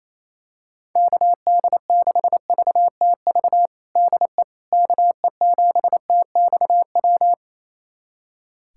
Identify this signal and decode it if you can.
Morse code, heard as a keyed tone, about 28 wpm, 700 Hz. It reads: KD6VTV DE KE7TXW
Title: USA Callsign DE Callsign 28wpm